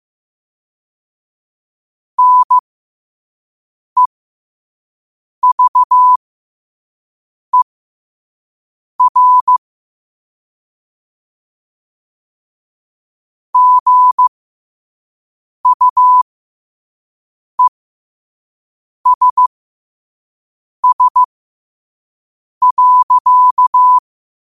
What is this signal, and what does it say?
Morse code, heard as a keyed tone, about 15 wpm, spaced as for 5 wpm, 1000 Hz. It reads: NEVER GUESS.
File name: qotdfiles / 5wpm